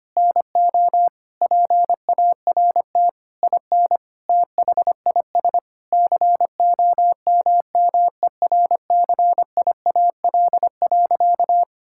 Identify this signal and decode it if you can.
Morse code, heard as a keyed tone, about 25 wpm, 700 Hz. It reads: NO PART IN THIS COMMERCIAL.
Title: Sentences from Top 1000 Words - Review Part 5 - 25wpm